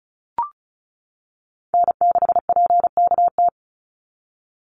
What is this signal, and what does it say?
Morse code, heard as a keyed tone, about 34 wpm, 700 Hz. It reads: N6PKT
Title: NA SOTA Chaser Callsigns 35wpm